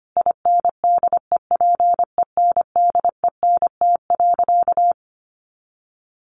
Morse code, heard as a keyed tone, about 25 wpm, 700 Hz. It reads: INDEPENDENT.
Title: Sentences from Top 900 Words 25wpm